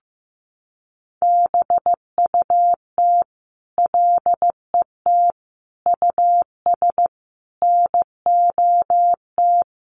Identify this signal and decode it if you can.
Morse code, heard as a keyed tone, about 15 wpm, 700 Hz. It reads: BUT LET US NOT
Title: Sentences from Top 1500 Words 15wpm